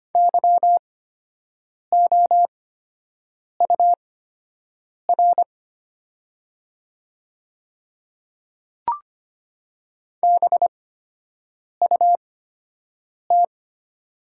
Morse code, heard as a keyed tone, about 25 wpm, spaced as for 7 wpm, 700 Hz. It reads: YOUR BUT
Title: 8x Character Spacing - Top 200 Words - 25wpm